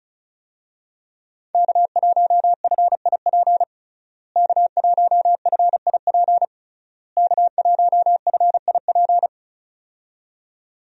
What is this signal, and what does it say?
Morse code, heard as a keyed tone, about 36 wpm, 700 Hz. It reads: K1FIP K1FIP K1FIP
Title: USA Call Signs - 5 Characters - Repeated 3x - 35wpm